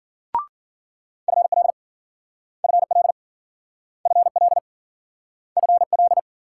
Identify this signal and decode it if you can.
Morse code, heard as a keyed tone, about 53 wpm, 700 Hz. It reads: FL FL FL FL